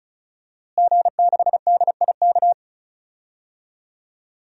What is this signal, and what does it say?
Morse code, heard as a keyed tone, about 36 wpm, 700 Hz. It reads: G6DIK
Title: DXCC-100 Call Signs - 5 Characters - Repeated 3x - 35wpm